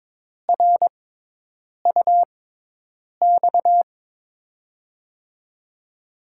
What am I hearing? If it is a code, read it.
Morse code, heard as a keyed tone, about 23 wpm, spaced as for 8 wpm, 700 Hz. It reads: RUX
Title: USA Call Signs - 6 Characters - 6x Character Spacing - 22wpm